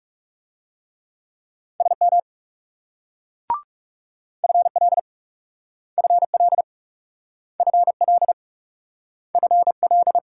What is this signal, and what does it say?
Morse code, heard as a keyed tone, about 38 wpm, 700 Hz. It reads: SM FL FL FL FL